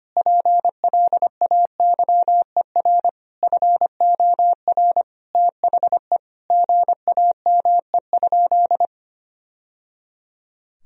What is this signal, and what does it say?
Morse code, heard as a keyed tone, about 25 wpm, 700 Hz. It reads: PLAYER FOR THE GAME?
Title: Sentences from Top 1000 Words - Review Part 4 - 25wpm